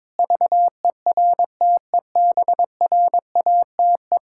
Morse code, heard as a keyed tone, about 22 wpm, 700 Hz. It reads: VERTEBRATE
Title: Pangram Challenge v2 - 22wpm